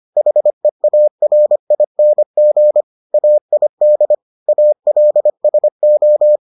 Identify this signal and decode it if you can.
Morse code, heard as a keyed tone, about 25 wpm, 600 Hz. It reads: HEARING AID ALSO